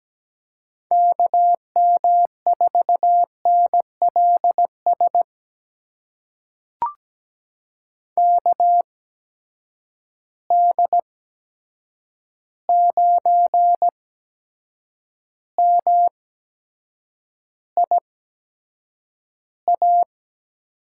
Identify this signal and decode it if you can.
Morse code, heard as a keyed tone, about 17 wpm, spaced as for 7 wpm, 700 Hz. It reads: KM4NLS KD9MIA
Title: USA Call Signs - 6 Characters - 8x Character Spacing - 17wpm